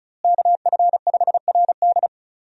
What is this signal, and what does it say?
Morse code, heard as a keyed tone, about 34 wpm, 700 Hz. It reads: KF5RD
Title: USA Call Signs - 5 Characters - Repeated 2x - 35wpm